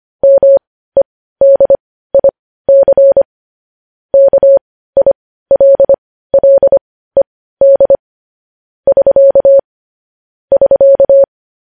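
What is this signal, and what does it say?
Morse code, heard as a keyed tone, about 25 wpm, spaced as for 15 wpm, 550 Hz. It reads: MEDIC KILLED <SK> <SK>
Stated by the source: Morse news headlines at 15 wpm.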